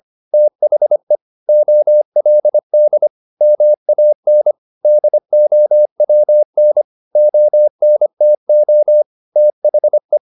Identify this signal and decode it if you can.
Morse code, heard as a keyed tone, about 25 wpm, 600 Hz. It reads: THE OLD MAN DOWN ONTO THE
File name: the old man and the sea part 1 25wpm